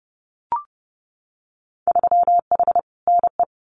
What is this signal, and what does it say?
Morse code, heard as a keyed tone, about 30 wpm, 700 Hz. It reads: 3H NE
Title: Winter Field Day Exchanges Encore 30wpm